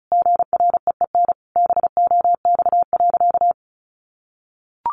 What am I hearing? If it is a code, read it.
Morse code, heard as a keyed tone, about 34 wpm, 700 Hz. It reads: GREEN BOX.
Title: Sentences from Top 500 Words 35wpm